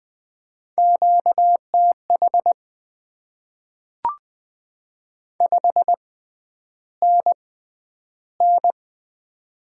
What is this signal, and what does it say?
Morse code, heard as a keyed tone, about 20 wpm, spaced as for 10 wpm, 690 Hz. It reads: QTH 5NN
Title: Top 20 QSO Elements - 6x Character Spacing - 20wpm